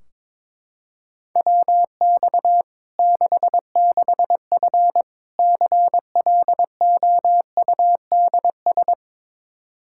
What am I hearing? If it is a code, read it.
Morse code, heard as a keyed tone, about 22 wpm, 700 Hz. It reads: WX 66F CLOUDS
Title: Practice QSO - Exchange 2 - Info - 22wpm